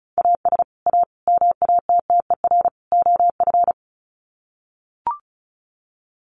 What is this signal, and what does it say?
Morse code, heard as a keyed tone, about 34 wpm, 700 Hz. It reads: AS A MATTER OF
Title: Sets of 4 Words 35wpm